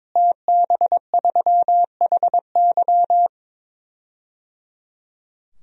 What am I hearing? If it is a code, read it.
Morse code, heard as a keyed tone, about 22 wpm, 700 Hz. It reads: TB3HY
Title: DXCC-100 Call Signs - 5 Characters - Repeated 3x - 22wpm